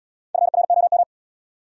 Morse code, heard as a keyed tone, about 80 wpm, 700 Hz. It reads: FROM